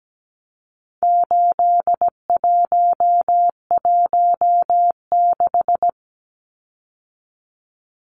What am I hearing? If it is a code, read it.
Morse code, heard as a keyed tone, about 17 wpm, 700 Hz. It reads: 8116
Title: Numbers - 4-Digit 17wpm